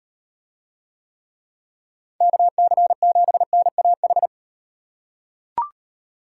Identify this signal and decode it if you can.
Morse code, heard as a keyed tone, about 38 wpm, 700 Hz. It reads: KC7NAH